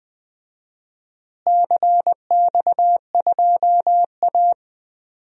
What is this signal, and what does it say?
Morse code, heard as a keyed tone, about 21 wpm, 700 Hz. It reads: CX2A
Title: DXCC-100 Call Signs - 4 Characters - Repeated 3x - 20wpm